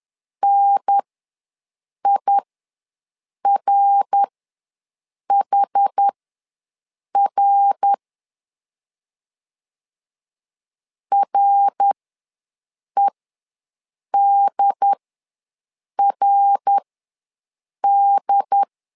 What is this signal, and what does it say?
Morse code, heard as a keyed tone, about 10 wpm, spaced as for 6 wpm, 800 Hz. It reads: NIRHR REDRD
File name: LEZIONI / 3 MWRO / 3 MWR ws 10wpm 2151